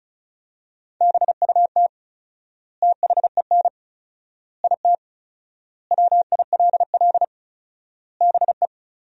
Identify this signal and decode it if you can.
Morse code, heard as a keyed tone, about 34 wpm, 700 Hz. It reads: BUT THEN IT WILL BE
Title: 4x Word-Spacing - Sentences from Top 300 Words - 35wpm